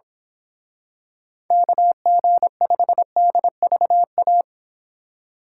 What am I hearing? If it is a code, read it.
Morse code, heard as a keyed tone, about 25 wpm, 700 Hz. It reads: KG5DVA
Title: Speed Racing - USA 2x3 Call Signs 22wpm